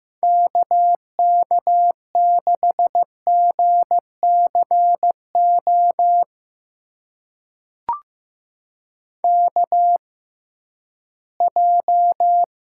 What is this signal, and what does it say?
Morse code, heard as a keyed tone, about 15 wpm, spaced as for 11 wpm, 700 Hz. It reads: KK6GCO KJ
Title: USA Call Signs - 6 Characters - 6x Character Spacing - 15wpm